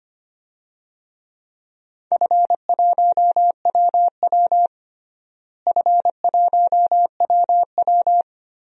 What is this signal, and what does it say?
Morse code, heard as a keyed tone, about 25 wpm, 700 Hz. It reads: F1WW F1WW
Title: DXCC-100 Call Signs - 4 Characters - Repeated 3x - 25wpm